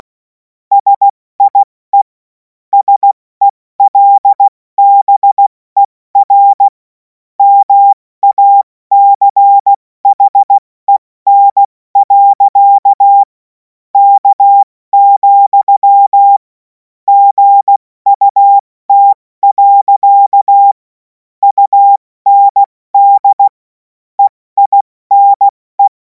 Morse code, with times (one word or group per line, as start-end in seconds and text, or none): 0.7-2.0 SIE
2.7-6.7 SELBER
7.4-13.2 MACHEN.
13.9-16.4 K,
17.1-20.7 GUT.
21.4-23.5 UND
24.2-25.9 EINE